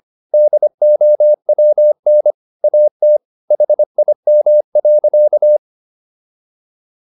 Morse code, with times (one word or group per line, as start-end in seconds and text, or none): 0.3-2.3 DOWN
2.6-3.2 AT
3.5-5.6 HIM.